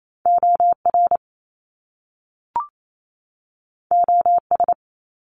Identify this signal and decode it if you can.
Morse code, heard as a keyed tone, about 28 wpm, 700 Hz. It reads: OR OS
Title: Top Two Letter Combinations in Words 28wpm